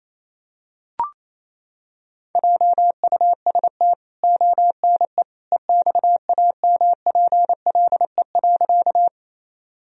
Morse code, heard as a keyed tone, about 28 wpm, 700 Hz. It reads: JUST ONE EXAMPLE.
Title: Sentences from Top 1000 Words - Review Part 1 - 28wpm